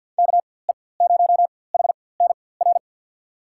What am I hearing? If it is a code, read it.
Morse code, heard as a keyed tone, about 52 wpm, spaced as for 23 wpm, 700 Hz. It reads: KE0HNR
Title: USA Call Signs - 6 Characters - 4x Character Spacing - 50wpm